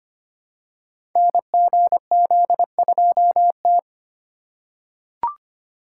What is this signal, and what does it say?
Morse code, heard as a keyed tone, about 25 wpm, 700 Hz. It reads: NGZ2T
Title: Code Groups - 5 Letter:Numbers - 25wpm